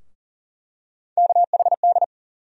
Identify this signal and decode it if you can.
Morse code, heard as a keyed tone, about 40 wpm, 700 Hz. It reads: KHD
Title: USA Call Sign Postfixes - 3 Letters 40wpm